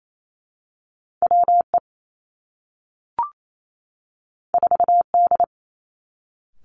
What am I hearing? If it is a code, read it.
Morse code, heard as a keyed tone, about 28 wpm, 700 Hz. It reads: W E 4 D
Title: Two Challenging Letter-Number Combinations 28wpm